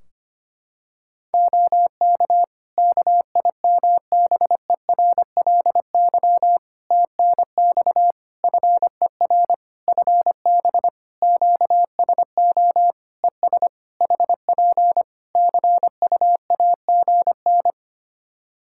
Morse code, heard as a keyed tone, about 25 wpm, 700 Hz. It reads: OK KIMBERLY TNX FER FB QSO ES HP CUAGN
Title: Practice QSO - Exchange 4 - Ending - 25wpm